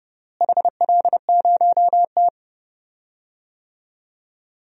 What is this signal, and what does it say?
Morse code, heard as a keyed tone, about 31 wpm, 700 Hz. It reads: HL0T
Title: DXCC-100 Call Signs - 4 Characters - Repeated 3x - 30wpm